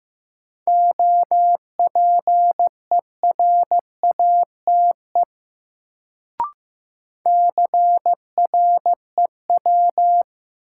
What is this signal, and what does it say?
Morse code, heard as a keyed tone, about 15 wpm, 700 Hz. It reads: OPERATE CREW
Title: Sentences from Top 1500 Words - New Word Drill - 15wpm